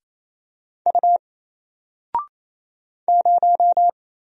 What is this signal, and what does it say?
Morse code, heard as a keyed tone, about 28 wpm, 700 Hz. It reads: U 0
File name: Single-Letter-Number-28wpm